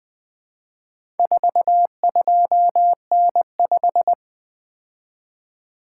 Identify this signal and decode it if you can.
Morse code, heard as a keyed tone, about 20 wpm, 700 Hz. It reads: 42N5
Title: Four Letter-Number Combinations - 20wpm